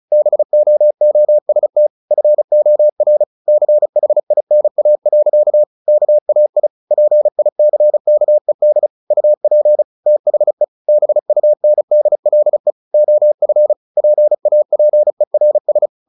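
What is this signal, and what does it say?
Morse code, heard as a keyed tone, about 34 wpm, 600 Hz. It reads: BOOST FOR CHINA. KAI PICKED UP THE BUNDLE OF PAPERS
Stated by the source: Morse practice at 35 wpm from CW med Gnister, sending Ken Follett's novel Never in English.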